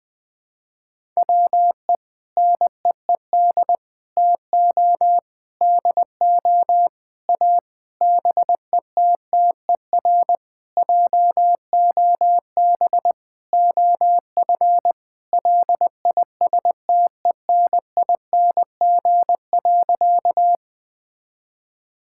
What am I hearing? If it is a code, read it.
Morse code, heard as a keyed tone, about 20 wpm, 700 Hz. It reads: WE NEED TO DO A BETTER JOB OF LISTENING.
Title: Sentences with 6-Word Phrases - 20wpm